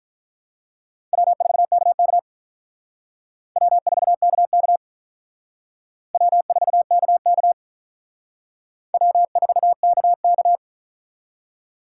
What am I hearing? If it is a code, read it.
Morse code, heard as a keyed tone, about 45 wpm, 700 Hz. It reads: W4KK W4KK W4KK W4KK